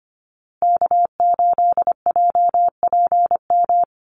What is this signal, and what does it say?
Morse code, heard as a keyed tone, about 25 wpm, 700 Hz. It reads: K8JPM
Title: Call Signs Repeated 2x - 25wpm